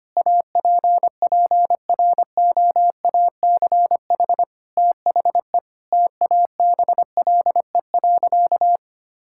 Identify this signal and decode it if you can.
Morse code, heard as a keyed tone, about 25 wpm, 700 Hz. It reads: APPROACH THE TABLE.
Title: Sentences from Top 1000 Words - Review Part 6 - 25wpm